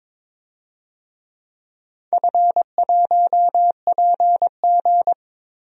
Morse code, heard as a keyed tone, about 22 wpm, 700 Hz. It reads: F1PG
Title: DXCC-100 Call Signs - 4 Characters - Repeated 2x - 22wpm